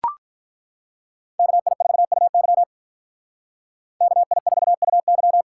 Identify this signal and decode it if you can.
Morse code, heard as a keyed tone, about 43 wpm, 700 Hz. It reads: KI4UY KI4UY
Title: Speed Racing - USA 2x2 Call Signs 35wpm